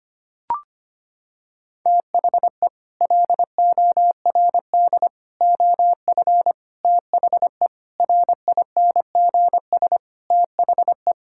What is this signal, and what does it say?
Morse code, heard as a keyed tone, about 25 wpm, 700 Hz. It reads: THE LORD OF THE RINGS THE
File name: Sets of 9 Words - 25wpm